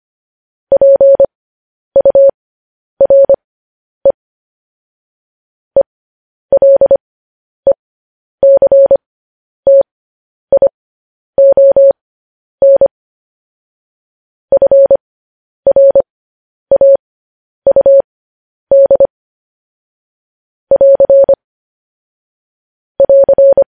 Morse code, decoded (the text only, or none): PURE ELECTION FRAUD <AR> <AR>